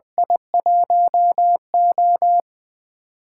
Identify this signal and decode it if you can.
Morse code, heard as a keyed tone, about 20 wpm, 700 Hz. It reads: I1O
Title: USA Call Signs - 4 Characters - 8x Character Spacing - 20wpm